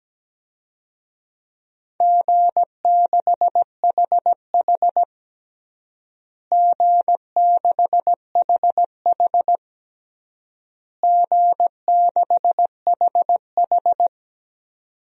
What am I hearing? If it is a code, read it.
Morse code, heard as a keyed tone, about 17 wpm, 700 Hz. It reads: G6HH G6HH G6HH
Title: DXCC-100 Call Signs - 4 Characters - Repeated 3x - 17wpm